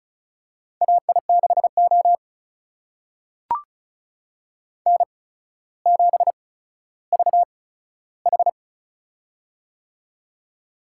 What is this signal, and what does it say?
Morse code, heard as a keyed tone, about 36 wpm, spaced as for 15 wpm, 700 Hz. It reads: AI6O N7VH
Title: USA Call Signs - 4 Characters - 8x Character Spacing - 35wpm